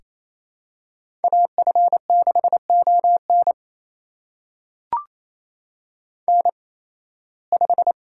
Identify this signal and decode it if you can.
Morse code, heard as a keyed tone, about 28 wpm, spaced as for 18 wpm, 700 Hz. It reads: AF6ON N5
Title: USA Call Signs - 8x Character Spacing - 28wpm